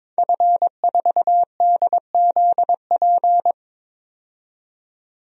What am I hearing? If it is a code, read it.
Morse code, heard as a keyed tone, about 22 wpm, 700 Hz. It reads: F4DZP
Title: DXCC-100 Call Signs - 5 Characters - 22wpm